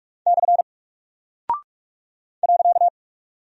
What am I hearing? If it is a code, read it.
Morse code, heard as a keyed tone, about 44 wpm, 700 Hz. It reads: / .
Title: Single Character (Letter:Number:Punctuation:Prosign) 45wpm